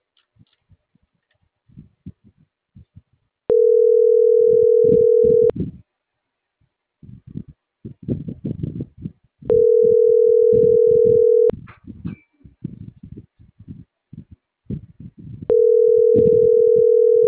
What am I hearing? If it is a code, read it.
Ringback tone (North America), 440 + 480 Hz.